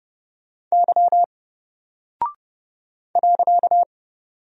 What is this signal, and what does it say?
Morse code, heard as a keyed tone, about 30 wpm, 700 Hz. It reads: Y .
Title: Single Character (Letter:Number:Punctuation:Prosign) 30wpm